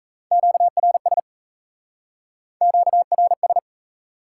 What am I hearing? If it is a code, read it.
Morse code, heard as a keyed tone, about 40 wpm, 700 Hz. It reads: QRS QRS